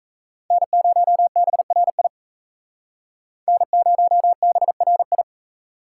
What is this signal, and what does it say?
Morse code, heard as a keyed tone, about 35 wpm, 700 Hz. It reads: N0BRI N0BRI